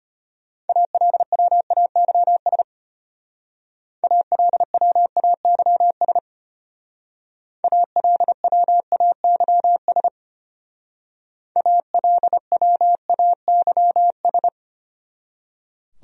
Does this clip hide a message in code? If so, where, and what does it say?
Yes: Morse code from 0.7 to 14.5 s, heard as a keyed tone, about 33 wpm, 700 Hz. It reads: ALWAYS ALWAYS ALWAYS ALWAYS